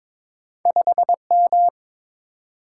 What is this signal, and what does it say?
Morse code, heard as a keyed tone, about 22 wpm, 700 Hz. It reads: 5M
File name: practice / 22wpm / Two Letter-Number Combinations - 22wpm